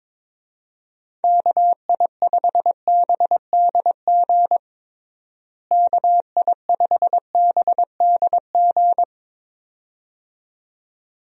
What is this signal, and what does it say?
Morse code, heard as a keyed tone, about 22 wpm, 700 Hz. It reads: KI5BDG KI5BDG
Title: USA Call Signs - 6 Characters - Repeated 2x - 22wpm